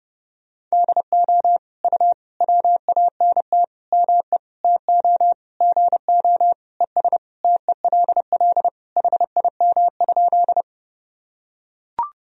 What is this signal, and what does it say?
Morse code, heard as a keyed tone, about 30 wpm, 700 Hz. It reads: DO U WANT ME TO GO ES TELL HIM?
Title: Sentences from Top 200 Words and Abbreviations 30wpm